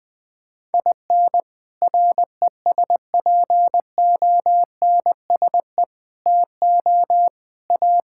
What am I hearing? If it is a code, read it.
Morse code, heard as a keyed tone, about 20 wpm, 700 Hz. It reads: IN RESPONSE TO A
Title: Sentences with 6-Word Phrases - 20wpm